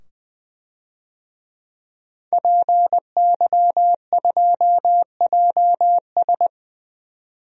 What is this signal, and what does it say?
Morse code, heard as a keyed tone, about 20 wpm, 700 Hz. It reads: PY2JS
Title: DXCC-100 Call Signs - 5 Characters - 20wpm